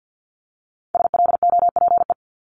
Morse code, heard as a keyed tone, about 50 wpm, 690 Hz. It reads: SLOPE
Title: Multi-letter Phonemes 50wpm